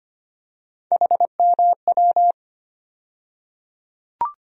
Morse code, heard as a keyed tone, about 25 wpm, 700 Hz. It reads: HMW
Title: USA Call Sign Postfixes - 3 Letters 25wpm